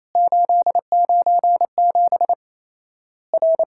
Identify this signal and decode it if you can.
Morse code, heard as a keyed tone, about 27 wpm, 700 Hz. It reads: 897 R
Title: Worked All Europe DX Contest - QTC Exchange - 28wpm